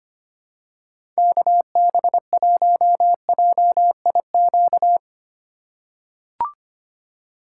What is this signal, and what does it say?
Morse code, heard as a keyed tone, about 25 wpm, 700 Hz. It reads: KB1JIQ
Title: USA Call Signs - 6 Characters - 4x Character Spacing - 25wpm